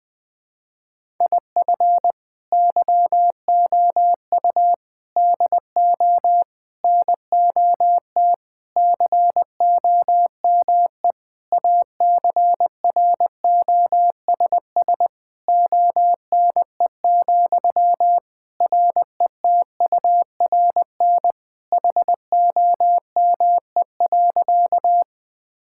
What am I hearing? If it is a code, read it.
Morse code, heard as a keyed tone, about 20 wpm, 700 Hz. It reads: IF YOU DO NOT COME ACROSS ONE, RETURN HOME.
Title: Sentences from Top 1000 Words - Review Part 5 - 20wpm